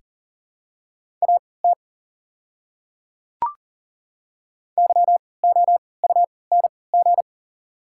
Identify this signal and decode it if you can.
Morse code, heard as a keyed tone, about 40 wpm, spaced as for 23 wpm, 690 Hz. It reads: AT YOUNG
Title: 3x Character Spacing - Top 500 Words - 40wpm